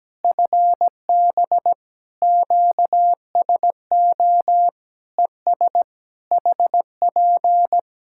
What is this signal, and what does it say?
Morse code, heard as a keyed tone, about 17 wpm, 700 Hz. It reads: FB QSO ES HP
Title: Practice QSO - Exchange 4 - Ending - 17wpm